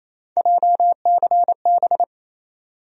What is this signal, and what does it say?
Morse code, heard as a keyed tone, about 28 wpm, 700 Hz. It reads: JCB
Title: USA Call Sign Postfixes - 3 Letters 28wpm